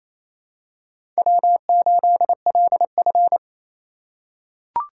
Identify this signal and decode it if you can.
Morse code, heard as a keyed tone, about 28 wpm, 700 Hz. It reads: W8LF
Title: USA Call Signs - 4 Characters - 3x Character Spacing - 28wpm